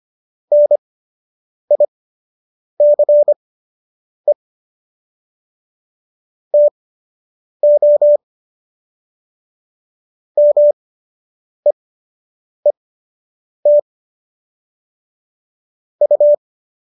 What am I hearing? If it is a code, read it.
Morse code, heard as a keyed tone, about 25 wpm, spaced as for 8 wpm, 590 Hz. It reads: NICE TO MEET U